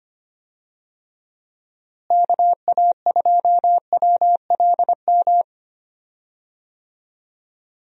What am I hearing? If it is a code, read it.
Morse code, heard as a keyed tone, about 25 wpm, 700 Hz. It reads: KA2WLM
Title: USA Call Signs - 6 Characters - Repeated 3x - 25wpm